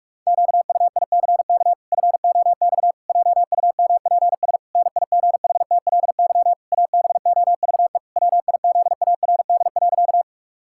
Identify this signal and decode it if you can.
Morse code, heard as a keyed tone, about 41 wpm, 700 Hz. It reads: QUICK FOX JUMPS NIGHTLY ABOVE WIZARD.
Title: Pangram Challenge v2 - 45wpm